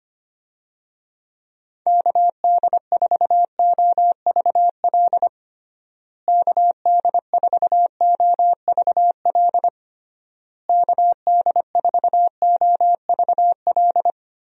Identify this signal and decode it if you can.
Morse code, heard as a keyed tone, about 25 wpm, 700 Hz. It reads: KD4OVL KD4OVL KD4OVL
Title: USA Call Signs - 6 Characters - Repeated 3x - 25wpm